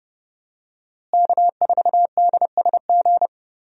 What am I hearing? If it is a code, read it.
Morse code, heard as a keyed tone, about 32 wpm, 700 Hz. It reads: K4DSG
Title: USA Call Signs - 5 Characters - Repeated 2x - 30wpm